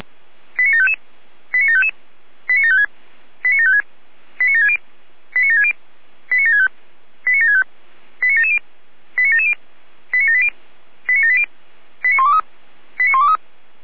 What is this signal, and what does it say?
Five-tone selective call (ZVEI), heard as quick runs of five tones, tones of about 70 ms. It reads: 89755 89755 89756 89756 89760 89760 89765 89765 89770 89770 89780 89780 89123 89123